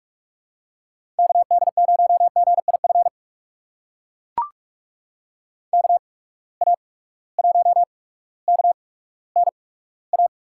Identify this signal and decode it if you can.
Morse code, heard as a keyed tone, about 43 wpm, spaced as for 19 wpm, 700 Hz. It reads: KD0GIF KA1KNA
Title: USA Call Signs - 6 Characters - 8x Character Spacing - 45wpm